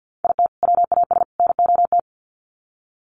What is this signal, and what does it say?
Morse code, heard as a keyed tone, about 49 wpm, 700 Hz. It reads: IT WAS NOT
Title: Sets of 3 Words 50wpm